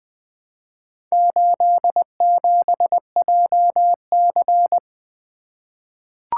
Morse code, heard as a keyed tone, about 20 wpm, 700 Hz. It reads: 87JC
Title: Four Letter-Number Combinations - 20wpm